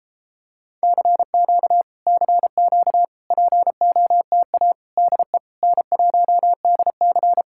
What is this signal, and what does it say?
Morse code, heard as a keyed tone, about 30 wpm, 700 Hz. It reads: CQ CQ POTA DE N1DC